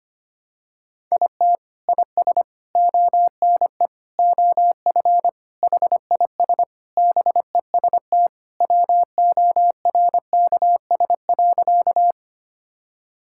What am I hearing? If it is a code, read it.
Morse code, heard as a keyed tone, about 25 wpm, 700 Hz. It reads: IT IS ONE OF HIS BEST WORKS.
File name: practice / 25wpm / Sentences with 3-Word Phrases - 25wpm